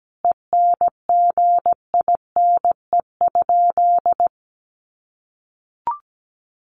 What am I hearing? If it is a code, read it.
Morse code, heard as a keyed tone, about 17 wpm, 700 Hz. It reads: ENGINE?
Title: Sentences from Top 1500 Words 17wpm